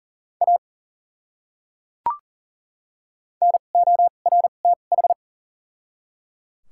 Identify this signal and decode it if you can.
Morse code, heard as a keyed tone, about 37 wpm, spaced as for 28 wpm, 700 Hz. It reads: A NORTH
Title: 2x Character Spacing - Top 300 Words - 40wpm